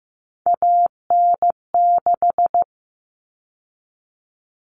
Morse code, heard as a keyed tone, about 15 wpm, 700 Hz. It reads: AN6
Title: Common Cut Numbers - 3-Digit 15wpm